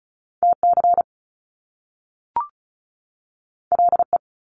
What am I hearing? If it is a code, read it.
Morse code, heard as a keyed tone, about 36 wpm, 700 Hz. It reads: TC LE
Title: Top Two Letter Combinations in Words 35wpm